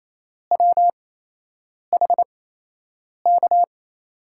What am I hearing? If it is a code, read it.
Morse code, heard as a keyed tone, about 28 wpm, spaced as for 8 wpm, 700 Hz. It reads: WHK
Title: USA Call Signs - 6 Characters - 8x Character Spacing - 28wpm